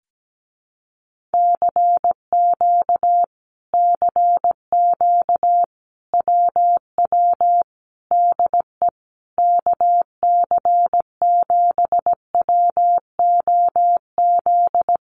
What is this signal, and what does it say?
Morse code, heard as a keyed tone, about 17 wpm, 700 Hz. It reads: CQ CQ WW DE KC7WOZ
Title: Common QSO Phrases Encore 17wpm